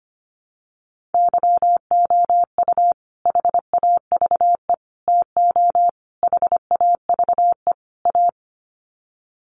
Morse code, heard as a keyed tone, about 25 wpm, 700 Hz. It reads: YOU HAVE TO HAVE A
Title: Sets of 5 Words Encore 25wpm